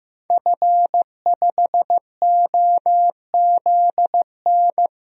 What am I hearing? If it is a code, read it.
Morse code, heard as a keyed tone, about 15 wpm, 700 Hz. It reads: F5OZN
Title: DXCC-100 Call Signs - 5 Characters - Repeated 3x - 15wpm